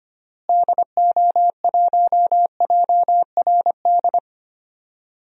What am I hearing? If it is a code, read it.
Morse code, heard as a keyed tone, about 25 wpm, 700 Hz. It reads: DO1JRD
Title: DXCC-100 Call Signs - 6 Characters - Repeated 3x - 25wpm